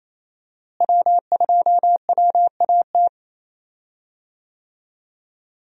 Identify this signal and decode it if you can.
Morse code, heard as a keyed tone, about 27 wpm, 700 Hz. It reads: W2WAT
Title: USA Call Signs - 5 Characters - Repeated 3x - 28wpm